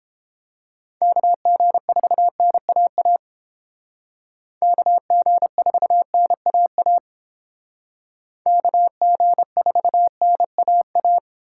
Morse code, heard as a keyed tone, about 30 wpm, 700 Hz. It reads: KG4NAA KG4NAA KG4NAA